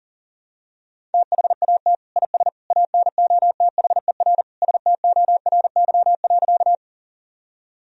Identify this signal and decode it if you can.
Morse code, heard as a keyed tone, about 40 wpm, 700 Hz. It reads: THAT IS ANOTHER STORY.
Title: Sentences from Top 1000 Words - Review Part 1 - 40wpm